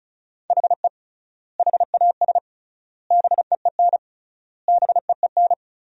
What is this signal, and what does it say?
Morse code, heard as a keyed tone, about 36 wpm, 700 Hz. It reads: HE HAS BEEN BEEN
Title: 3x Word-Spacing - Sentences from Top 300 Words - 35wpm